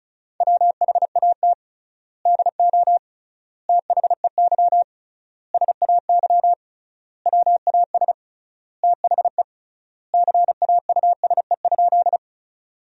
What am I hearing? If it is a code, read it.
Morse code, heard as a keyed tone, about 36 wpm, 700 Hz. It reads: WHAT DO THEY SAY WAS THE CAUSE?
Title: 3x Word-Spacing - Sentences from Top 200 Words - 35wpm